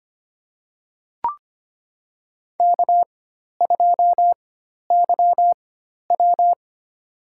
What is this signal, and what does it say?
Morse code, heard as a keyed tone, about 25 wpm, spaced as for 12 wpm, 700 Hz. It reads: K2YW
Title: USA Call Signs - 4 Characters - 4x Character Spacing - 25wpm